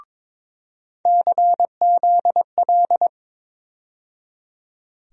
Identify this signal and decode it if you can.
Morse code, heard as a keyed tone, about 22 wpm, 700 Hz. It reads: CZL
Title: USA Call Sign Postfixes - 3 Letters 22wpm